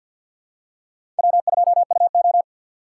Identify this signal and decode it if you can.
Morse code, heard as a keyed tone, about 52 wpm, 700 Hz. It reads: W2UO
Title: USA Call Signs - 6x Character Spacing - 50wpm